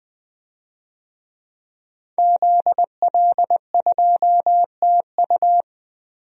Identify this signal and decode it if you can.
Morse code, heard as a keyed tone, about 20 wpm, 700 Hz. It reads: ZL2TU
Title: DXCC-100 Call Signs - 5 Characters - Repeated 3x - 20wpm